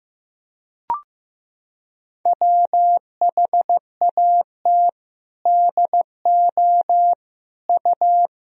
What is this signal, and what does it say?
Morse code, heard as a keyed tone, about 15 wpm, 700 Hz. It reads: WHAT DO U
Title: Sentences from Top 200 Words and Abbreviations 15wpm